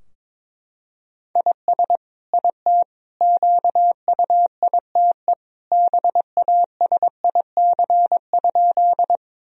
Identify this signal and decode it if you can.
Morse code, heard as a keyed tone, about 22 wpm, 700 Hz. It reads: IS IT QUITE BASIC?
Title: Sentences from Top 1000 Words - Review Part 3 - 22wpm